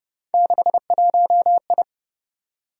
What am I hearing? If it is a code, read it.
Morse code, heard as a keyed tone, about 32 wpm, 700 Hz. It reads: 61I